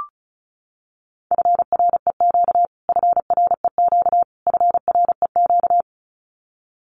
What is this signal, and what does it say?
Morse code, heard as a keyed tone, about 34 wpm, 700 Hz. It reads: FREQ FREQ FREQ
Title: Abbreviations Repeated 3x - 35wpm